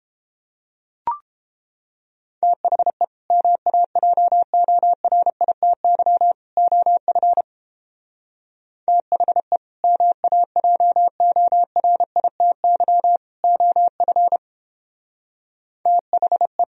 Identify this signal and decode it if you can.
Morse code, heard as a keyed tone, about 31 wpm, 700 Hz. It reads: THE MAJORITY OF THE MAJORITY OF THE